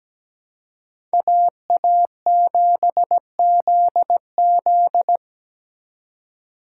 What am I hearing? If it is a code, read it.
Morse code, heard as a keyed tone, about 20 wpm, 700 Hz. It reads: AA7ZZ
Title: Speed Racing - USA 2x2 Call Signs 17wpm